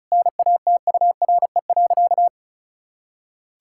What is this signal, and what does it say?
Morse code, heard as a keyed tone, about 34 wpm, 700 Hz. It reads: NATURE.
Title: Sentences from Top 1000 Words - Review Part 5 - 35wpm